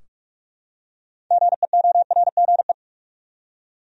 Morse code, heard as a keyed tone, about 46 wpm, 700 Hz. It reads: GEORGE
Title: USA Common Names - 45wpm